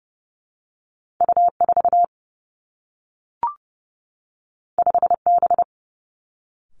Morse code, heard as a keyed tone, about 30 wpm, 700 Hz. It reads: U4 5B
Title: Two Challenging Letter-Number Combinations 30wpm